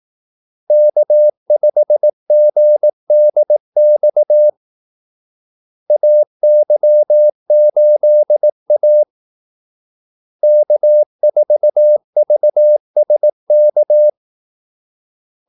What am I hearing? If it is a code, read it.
Morse code, heard as a keyed tone, about 18 wpm, 600 Hz. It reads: K5GDX AY8A K4VSK
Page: Kaldesignaler 18 wpm